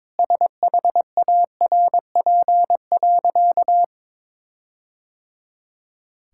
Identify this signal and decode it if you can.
Morse code, heard as a keyed tone, about 22 wpm, 700 Hz. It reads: SHARP.
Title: Sentences from Top 1600 Words 22wpm